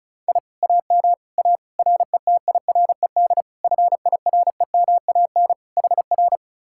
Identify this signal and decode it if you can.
Morse code, heard as a keyed tone, about 44 wpm, 700 Hz. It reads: I AM A RETIRED FIREMAN HR